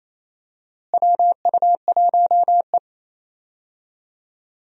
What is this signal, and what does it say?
Morse code, heard as a keyed tone, about 28 wpm, 700 Hz. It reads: WU1E
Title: USA Call Signs - 4 Characters - Repeated 3x - 28wpm